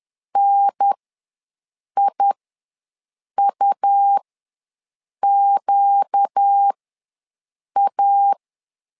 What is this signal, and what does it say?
Morse code, heard as a keyed tone, about 10 wpm, spaced as for 6 wpm, 800 Hz. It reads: NIUQA
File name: LEZIONI / 4 Q ws 10wpm 2306